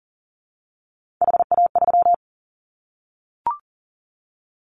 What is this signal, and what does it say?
Morse code, heard as a keyed tone, about 40 wpm, 700 Hz. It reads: HA3
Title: Callsign Prefixes for Top 25 DXCC - 40wpm